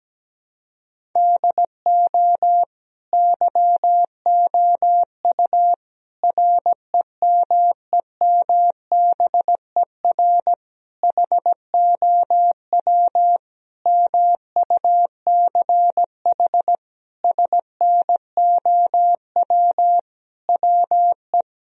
Morse code, heard as a keyed tone, about 17 wpm, 700 Hz. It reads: DO YOU REMEMBER HOW MUCH SNOW WE
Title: Sentences from Top 1000 Words - Review Part 3 - 17wpm